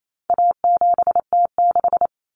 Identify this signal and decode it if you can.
Morse code, heard as a keyed tone, about 28 wpm, 700 Hz. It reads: A7T6
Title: Common Cut Numbers - 4-Digit 28wpm